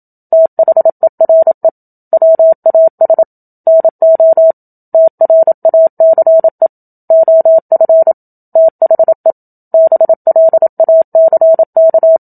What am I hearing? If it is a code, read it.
Morse code, heard as a keyed tone, about 27 wpm, 650 Hz. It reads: THERE WAS NO TRACE OF THE BLACK
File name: Book Two / 01-Chapter 1 - 27 WPM